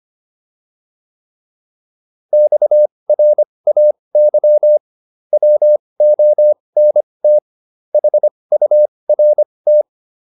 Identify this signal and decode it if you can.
Morse code, heard as a keyed tone, about 25 wpm, spaced as for 20 wpm, 600 Hz. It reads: XRAY WONT HURT